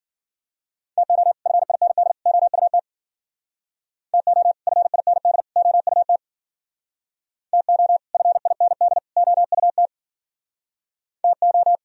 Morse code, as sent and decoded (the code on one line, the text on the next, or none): - --- / ..-. .. -. -.. / --- ..- - / - --- / ..-. .. -. -.. / --- ..- - / - --- / ..-. .. -. -.. / --- ..- - / - ---
TO FIND OUT TO FIND OUT TO FIND OUT TO